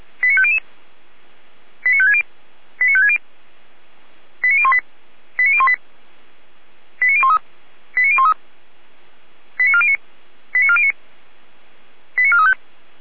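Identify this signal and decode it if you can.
Five-tone selective call (ZVEI), heard as quick runs of five tones, tones of about 70 ms. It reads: 89500 89560 89560 89018 89018 89013 89013 89409 89409 89435
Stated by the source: ZVEI five-tone selective call received on an SDR, feuerwehr-orlamuende.org fire brigade audio server.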